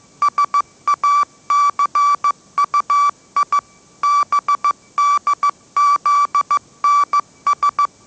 Morse code, heard as a keyed tone, about 15 wpm, 1160 Hz. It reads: SACUI BDZNS
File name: mrx / mrx066 16wpm